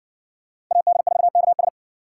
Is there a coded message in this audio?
Yes: Morse code from 0.7 to 1.7 s, heard as a keyed tone, about 60 wpm, 700 Hz. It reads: AD4KS